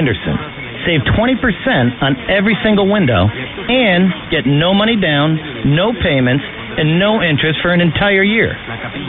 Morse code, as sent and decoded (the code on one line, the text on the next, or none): none